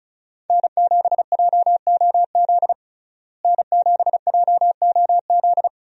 Morse code, heard as a keyed tone, about 34 wpm, 700 Hz. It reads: N7JOZ N7JOZ
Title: USA Call Signs - 5 Characters - Repeated 2x - 35wpm